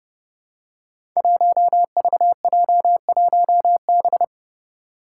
Morse code, heard as a keyed tone, about 30 wpm, 700 Hz. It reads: 1VJ1B
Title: Five Challenging Letter-Number Combinations 30wpm